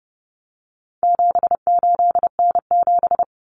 Morse code, heard as a keyed tone, about 30 wpm, 700 Hz. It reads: 78N7
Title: Common Cut Numbers - 4-Digit 30wpm